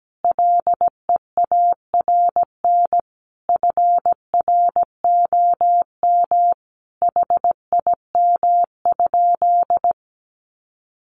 Morse code, heard as a keyed tone, about 17 wpm, 700 Hz. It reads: LEARN FROM HIM?
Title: Sentences from Top 300 Words 17wpm